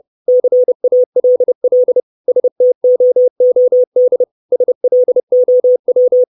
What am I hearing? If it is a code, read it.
Morse code, heard as a keyed tone, about 30 wpm, 500 Hz. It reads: CALL STOOD SLOW
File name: 632 Words 30WPM